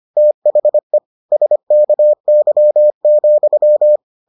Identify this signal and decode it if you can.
Morse code, heard as a keyed tone, about 25 wpm, 600 Hz. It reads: THE SKY,